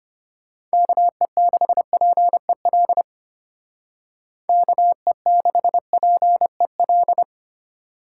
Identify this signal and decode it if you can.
Morse code, heard as a keyed tone, about 30 wpm, 700 Hz. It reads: KE6PEL KE6PEL
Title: Speed Racing - USA 2x3 Call Signs 25wpm